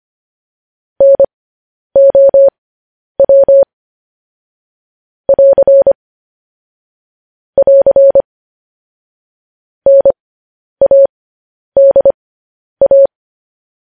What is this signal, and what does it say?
Morse code, heard as a keyed tone, about 25 wpm, spaced as for 10 wpm, 550 Hz. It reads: NOW <AR> <AR> NADA
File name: podcast / News headlines-1x-25x10